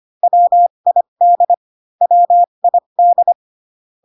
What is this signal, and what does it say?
Morse code, heard as a keyed tone, about 25 wpm, spaced as for 22 wpm, 700 Hz. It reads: WID WID
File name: CW common abbreviations alphabetical repeated 3 times 25 18wpm